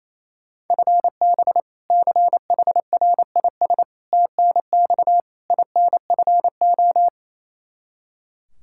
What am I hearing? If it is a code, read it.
Morse code, heard as a keyed tone, about 28 wpm, 700 Hz. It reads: FB CHRIS TNX INFO
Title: Practice QSO - Exchange 3 - More Info - 28wpm